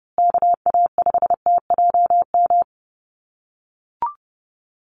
Morse code, heard as a keyed tone, about 30 wpm, 700 Hz. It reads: KA5TJM
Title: USA Call Signs Repeated 3x 30wpm